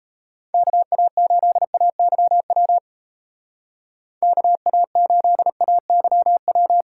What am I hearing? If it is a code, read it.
Morse code, heard as a keyed tone, about 33 wpm, 700 Hz. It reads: KA8AYW KA8AYW
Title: Speed Racing - USA 2x3 Call Signs 28wpm